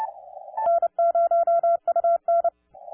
Morse code, heard as a keyed tone, about 29 wpm, 690 Hz. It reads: N0UN